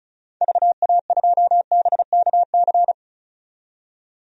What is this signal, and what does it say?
Morse code, heard as a keyed tone, about 34 wpm, 700 Hz. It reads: VA2BKC
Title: DXCC-100 Call Signs - 6 Characters - 35wpm